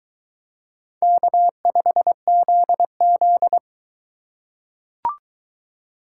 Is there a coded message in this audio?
Yes: Morse code from 1.0 to 3.6 s, heard as a keyed tone, about 22 wpm, 700 Hz. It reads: K5ZZ